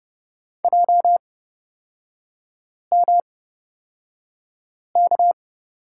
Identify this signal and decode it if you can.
Morse code, heard as a keyed tone, about 30 wpm, spaced as for 5 wpm, 700 Hz. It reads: JMK
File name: Lesson 36e - BT - Three Characters 30wpm